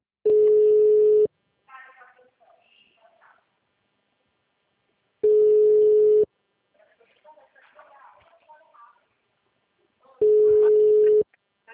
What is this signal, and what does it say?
Ringback tone (Europe, 425 Hz), with speech.
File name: out- -2049- - - .59475